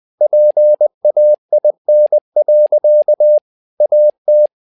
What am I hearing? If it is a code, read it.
Morse code, heard as a keyed tone, about 20 wpm, 600 Hz. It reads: PAIN. AT
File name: learning center / tower trouble / tower trouble 20wpm